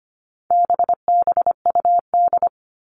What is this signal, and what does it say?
Morse code, heard as a keyed tone, about 25 wpm, 700 Hz. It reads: BBUD
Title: All Cut Numbers - 4-Digit 25wpm